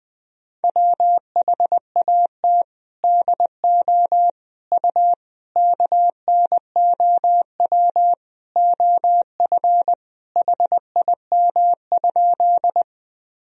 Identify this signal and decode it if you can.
Morse code, heard as a keyed tone, about 21 wpm, 700 Hz. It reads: WHAT DO U KNOW OF HIM?
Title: Sentences from Top 100 Words and Abbreviations 20wpm